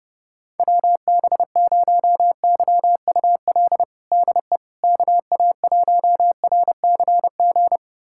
Morse code, heard as a keyed tone, about 31 wpm, 700 Hz. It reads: WB0YUL DE KA1RCG
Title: USA Callsign DE Callsign 30wpm